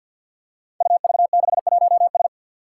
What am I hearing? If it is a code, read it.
Morse code, heard as a keyed tone, about 52 wpm, 700 Hz. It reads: UV61S
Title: Five Challenging Letter-Number Combinations 50wpm